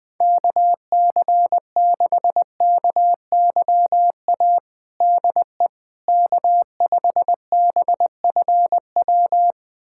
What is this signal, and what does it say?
Morse code, heard as a keyed tone, about 21 wpm, 700 Hz. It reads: KC6KYA DE K5BFW
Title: USA Callsign DE Callsign 20wpm